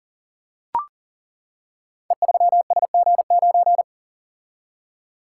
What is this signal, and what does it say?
Morse code, heard as a keyed tone, about 39 wpm, 700 Hz. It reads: E3SG9
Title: Code Groups - 5 Letter:Numbers - 40wpm